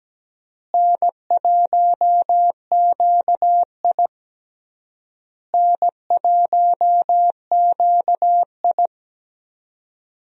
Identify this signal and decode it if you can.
Morse code, heard as a keyed tone, about 17 wpm, 700 Hz. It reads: N1QI N1QI
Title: USA Call Signs - 4 Characters - Repeated 3x - 17wpm